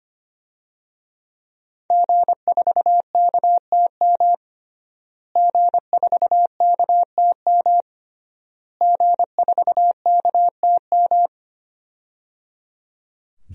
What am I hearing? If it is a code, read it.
Morse code, heard as a keyed tone, about 25 wpm, 700 Hz. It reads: G4KTM G4KTM G4KTM
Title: DXCC-100 Call Signs - 5 Characters - Repeated 3x - 25wpm